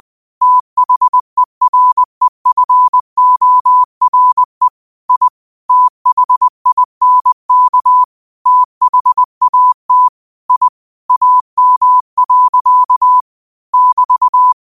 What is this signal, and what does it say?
Morse code, heard as a keyed tone, about 20 wpm, 1000 Hz. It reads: THEREFORE I THINK THAT I AM. <BT>
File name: qotdfiles / 20wpm